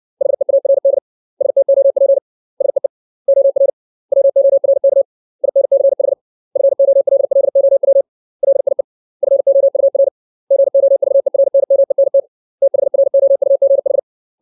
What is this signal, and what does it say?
Morse code, heard as a keyed tone, about 60 wpm, 550 Hz. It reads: HEARD STOP SEE OR WORK INCH FOLLOW LIE FOUR GOVERNMENT THROUGH
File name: 632 words / 632 Words 60WPM